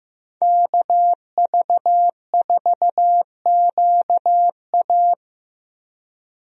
Morse code, heard as a keyed tone, about 15 wpm, 700 Hz. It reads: KV4QA
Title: USA Call Signs - 5 Characters - Repeated 3x - 15wpm